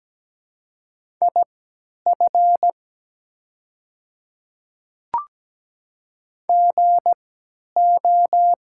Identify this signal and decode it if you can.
Morse code, heard as a keyed tone, about 17 wpm, spaced as for 10 wpm, 700 Hz. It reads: IF GO
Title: 3x Character Spacing - Top 100 Words - 17wpm